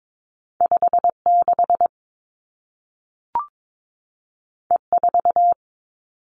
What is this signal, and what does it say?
Morse code, heard as a keyed tone, about 22 wpm, 700 Hz. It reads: 5 6 E 4
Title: Two Challenging Letter-Number Combinations 22wpm